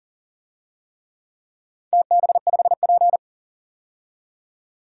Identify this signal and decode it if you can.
Morse code, heard as a keyed tone, about 40 wpm, 700 Hz. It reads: TB5P
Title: DXCC-100 Call Signs - 4 Characters - 40wpm